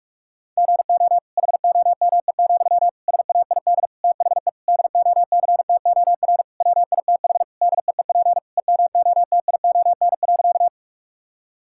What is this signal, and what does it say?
Morse code, heard as a keyed tone, about 46 wpm, 700 Hz. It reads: GO HOME, SAID THE DOCTOR WITH DEEP EMOTION.
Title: Sentences from Top 1600 Words 45wpm